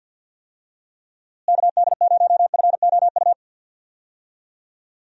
Morse code, heard as a keyed tone, about 52 wpm, 700 Hz. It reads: KD0FOU
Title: USA Call Signs - 6 Characters - Repeated 2x - 50wpm